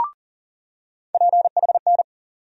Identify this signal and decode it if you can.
Morse code, heard as a keyed tone, about 40 wpm, 700 Hz. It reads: PHN
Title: USA Call Sign Postfixes - 3 Letters 40wpm